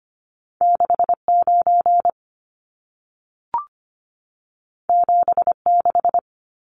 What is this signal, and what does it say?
Morse code, heard as a keyed tone, about 25 wpm, 700 Hz. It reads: 69 76
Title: Numbers - 2-Digit 25wpm